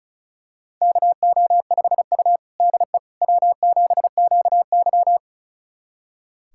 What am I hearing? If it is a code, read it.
Morse code, heard as a keyed tone, about 36 wpm, 700 Hz. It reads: KO5U DE W7QY
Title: USA Callsign DE Callsign - 4 Characters - 35wpm